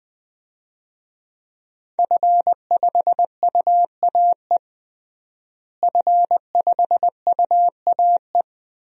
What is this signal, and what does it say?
Morse code, heard as a keyed tone, about 20 wpm, 700 Hz. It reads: F5UAE F5UAE
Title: DXCC-100 Call Signs - 5 Characters - Repeated 3x - 20wpm